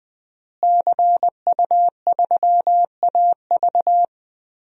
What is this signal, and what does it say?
Morse code, heard as a keyed tone, about 20 wpm, 700 Hz. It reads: CU3AV
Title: DXCC-100 Call Signs - 5 Characters - Repeated 2x - 20wpm